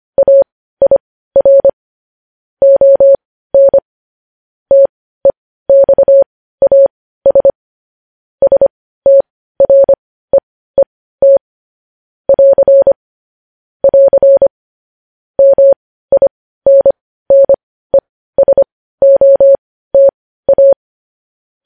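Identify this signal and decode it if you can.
Morse code, heard as a keyed tone, about 25 wpm, spaced as for 15 wpm, 550 Hz. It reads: AIR ON TEXAS STREET <AR> <AR> MINNESOTA
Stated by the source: Morse code news headlines sent at 15 words per minute.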